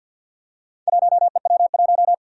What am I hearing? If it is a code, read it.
Morse code, heard as a keyed tone, about 51 wpm, 700 Hz. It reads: 1EW1
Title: Four Challenging Letter-Number Combinations 50wpm